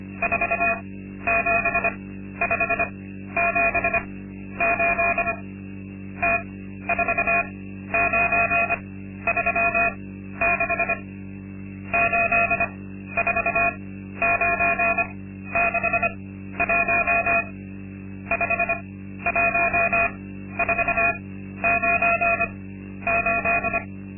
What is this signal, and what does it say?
Morse code, heard as a keyed tone, about 25 wpm, spaced as for 12 wpm, 680 Hz. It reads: 47578 T4936 84961 51498
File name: Linksmakalnis-2